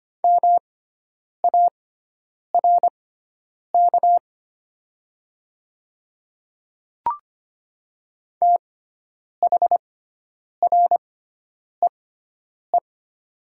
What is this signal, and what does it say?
Morse code, heard as a keyed tone, about 25 wpm, spaced as for 9 wpm, 700 Hz. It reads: MARK THREE
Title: 6x Character Spacing - Top 500 Words - 25wpm